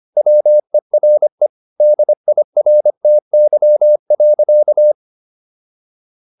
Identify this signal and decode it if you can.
Morse code, heard as a keyed tone, about 25 wpm, 600 Hz. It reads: WERE DIRTY.